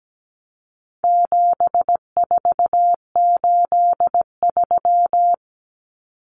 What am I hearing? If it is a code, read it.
Morse code, heard as a keyed tone, about 17 wpm, 700 Hz. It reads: 7483
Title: Common Cut Numbers - 4-Digit 17wpm